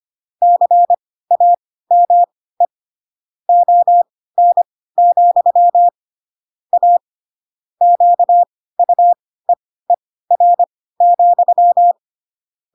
Morse code, heard as a keyed tone, about 25 wpm, spaced as for 16 wpm, 700 Hz. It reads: CAME ON, A QUEER,
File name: Pinocchio / Pinocchio ch5 10 25 ik1hge